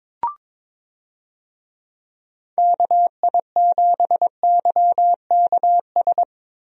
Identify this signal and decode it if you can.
Morse code, heard as a keyed tone, about 22 wpm, 700 Hz. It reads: KI7YKS
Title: USA Call Signs - 6 Characters - Repeated 3x - 22wpm